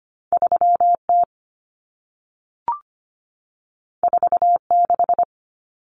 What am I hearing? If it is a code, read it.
Morse code, heard as a keyed tone, about 25 wpm, 700 Hz. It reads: 3T 46
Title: Common Cut Numbers - 2-Digit 25wpm